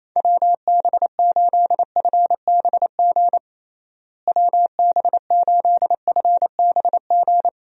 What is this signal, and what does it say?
Morse code, heard as a keyed tone, about 28 wpm, 700 Hz. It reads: WB8FBG WB8FBG
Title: USA Call Signs - 6 Characters - Repeated 2x - 28wpm